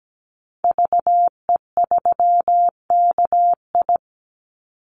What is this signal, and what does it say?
Morse code, heard as a keyed tone, about 17 wpm, 700 Hz. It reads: VE3KI
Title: Call Signs - 17wpm